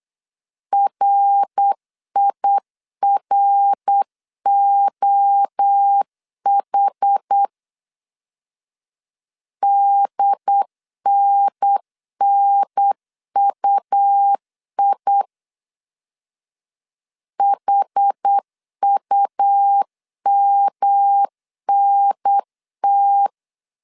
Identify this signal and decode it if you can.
Morse code, heard as a keyed tone, about 8 wpm, 800 Hz. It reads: RIROH DNNUI HUMNT
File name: CW / DET / 1 3 1500C S B 8wpm 4141